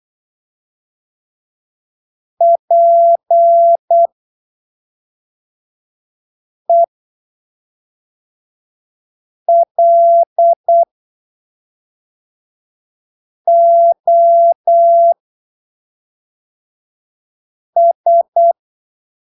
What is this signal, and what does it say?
Morse code, heard as a keyed tone, about 8 wpm, spaced as for 3 wpm, 680 Hz. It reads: PELOS
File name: arquivos / texto / texto 06 08ppm